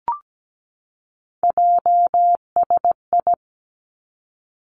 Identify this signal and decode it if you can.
Morse code, heard as a keyed tone, about 17 wpm, 700 Hz. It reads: JSI
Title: Three Challenging Letter-Number Combinations 17wpm